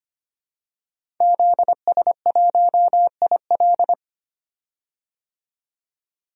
Morse code, heard as a keyed tone, about 25 wpm, 700 Hz. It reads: ZS1IL
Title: DXCC-100 Call Signs - 5 Characters - 25wpm